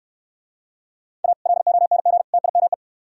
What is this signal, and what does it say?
Morse code, heard as a keyed tone, about 66 wpm, 700 Hz. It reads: A LONG TIME